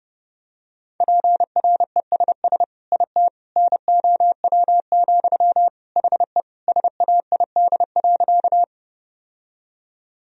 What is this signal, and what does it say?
Morse code, heard as a keyed tone, about 32 wpm, 700 Hz. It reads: PRESS IT NOW, HE SAID.
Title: Sentences from Top 1000 Words - Review Part 1 - 30wpm